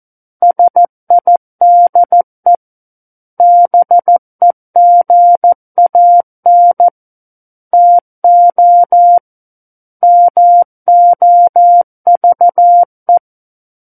Morse code, heard as a keyed tone, about 14 wpm, 700 Hz. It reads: SIDE BEGAN TO MOVE
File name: Book One / 05-Chapter 5 - 14 WPM